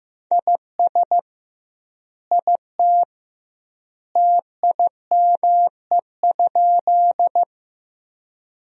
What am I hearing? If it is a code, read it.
Morse code, heard as a keyed tone, about 15 wpm, 700 Hz. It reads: IS IT TIME?
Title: 2x Word-Spacing - Sentences from Top 100 Words - 15wpm